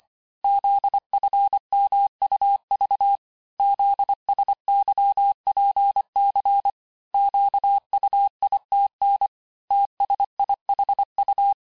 Morse code, heard as a keyed tone, about 25 wpm, 800 Hz. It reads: ZFMUV ZSYPC QUITN TSIHU
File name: texte-chiffre-vitesse-23mpm